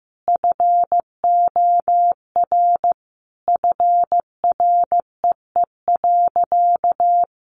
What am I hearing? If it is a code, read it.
Morse code, heard as a keyed tone, about 15 wpm, 700 Hz. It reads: FOR FREE.
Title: Additional Easy Phrases 15wpm